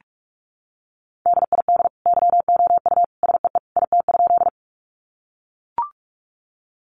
Morse code, heard as a keyed tone, about 44 wpm, 700 Hz. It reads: DID YOU SEE IT?
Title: Sentences from Top 100 Words 45wpm